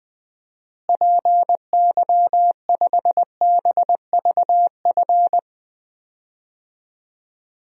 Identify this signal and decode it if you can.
Morse code, heard as a keyed tone, about 20 wpm, 700 Hz. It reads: PY5BVF
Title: DXCC-100 Call Signs - 6 Characters - Repeated 3x - 20wpm